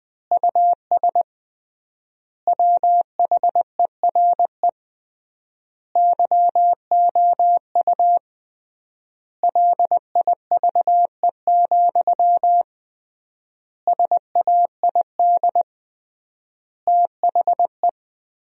Morse code, heard as a keyed tone, about 21 wpm, 700 Hz. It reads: US WHERE YOU LIVE, SAID THE
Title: 3x Word-Spacing - Sentences from Top 200 Words - 20wpm